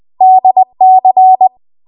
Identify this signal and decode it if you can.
Morse code, heard as a keyed tone, about 20 wpm, 750 Hz. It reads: DC
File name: Archive / 20 WPM / 20WPM